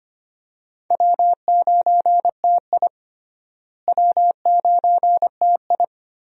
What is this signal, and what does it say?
Morse code, heard as a keyed tone, about 25 wpm, 700 Hz. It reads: W9TI W9TI
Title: USA Call Signs - 4 Characters - Repeated 3x - 25wpm